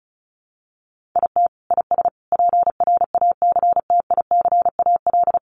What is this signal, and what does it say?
Morse code, heard as a keyed tone, about 34 wpm, 700 Hz. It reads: IT IS PRACTICAL
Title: Sentences from Top 1400 Words 35wpm